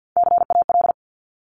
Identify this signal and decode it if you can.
Morse code, heard as a keyed tone, about 50 wpm, 700 Hz. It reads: CAL
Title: Top Three Letter Combinations in Words 50wpm